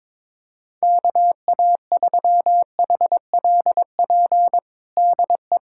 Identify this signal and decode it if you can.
Morse code, heard as a keyed tone, about 22 wpm, 700 Hz. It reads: KA3HLP DE
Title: USA Callsign DE Callsign 22wpm